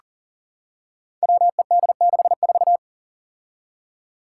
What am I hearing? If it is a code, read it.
Morse code, heard as a keyed tone, about 40 wpm, 700 Hz. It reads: WED64
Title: Five Challenging Letter-Number Combinations 40wpm